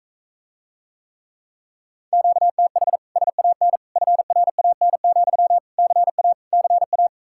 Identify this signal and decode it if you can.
Morse code, heard as a keyed tone, about 45 wpm, 700 Hz. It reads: QTH SAN FRAN, CA CA